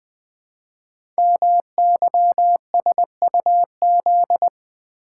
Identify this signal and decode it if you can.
Morse code, heard as a keyed tone, about 20 wpm, 700 Hz. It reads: MYSUZ
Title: Code Groups - 5 Letters - 20wpm